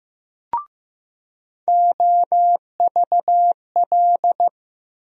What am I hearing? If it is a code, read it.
Morse code, heard as a keyed tone, about 15 wpm, 700 Hz. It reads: OVL
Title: USA Call Sign Postfixes - 3 Letters 15wpm